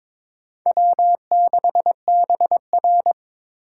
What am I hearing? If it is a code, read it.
Morse code, heard as a keyed tone, about 23 wpm, 700 Hz. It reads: W6BR
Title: USA Call Signs - 4 Characters - 2x Character Spacing - 22wpm